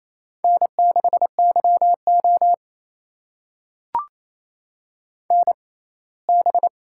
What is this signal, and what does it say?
Morse code, heard as a keyed tone, about 28 wpm, spaced as for 19 wpm, 700 Hz. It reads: N6YO NB
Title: USA Call Signs - 4 Characters - 6x Character Spacing - 28wpm